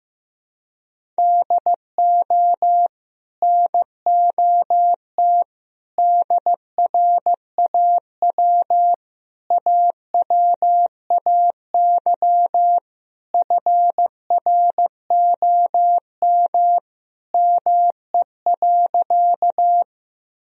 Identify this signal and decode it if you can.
Morse code, heard as a keyed tone, about 15 wpm, 700 Hz. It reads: DO NOT DRAW AWAY FROM ME.
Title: Sentences from Top 1000 Words - Review Part 1 - 15wpm